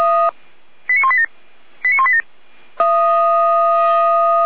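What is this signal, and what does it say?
Five-tone selective call (ZVEI), heard as quick runs of five tones, tones of about 70 ms. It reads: 89187 89187
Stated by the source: ZVEI five-tone selective call 89140 received with an SDR by Feuerwehr Orlamünde.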